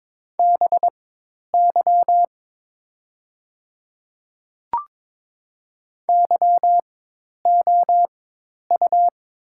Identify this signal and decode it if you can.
Morse code, heard as a keyed tone, about 22 wpm, spaced as for 10 wpm, 700 Hz. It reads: BY YOU
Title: 4x Character Spacing - Top 200 Words - 22wpm